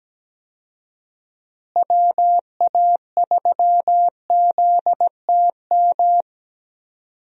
Morse code, heard as a keyed tone, about 17 wpm, 700 Hz. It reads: WA3ZTM
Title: USA Call Signs - 6 Characters - Repeated 2x - 17wpm